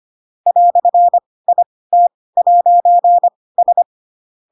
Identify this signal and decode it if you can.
Morse code, heard as a keyed tone, about 25 wpm, spaced as for 18 wpm, 700 Hz. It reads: "IT'S
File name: Pinocchio / Pinocchio ch2 12 25 ik1hge